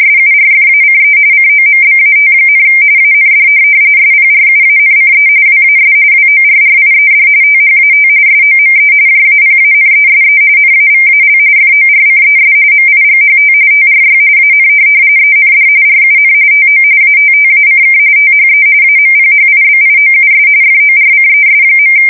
RTTY, about 45.45 baud. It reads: SURFACE COMBAT, AND SPRUANCE WAS WORRIED THAT SOME JAPANESE NAVAL FORCES MIGHT TRY AN END RUN TO ATTACK THE AMPHIBIOUS FORCES.
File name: fsk170 / USS-New-Jersey